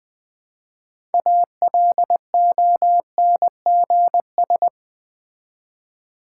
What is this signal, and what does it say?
Morse code, heard as a keyed tone, about 20 wpm, 700 Hz. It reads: ALONGS
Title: Build A Word - 3 Letters At A Time - 9-Letter Words - 20wpm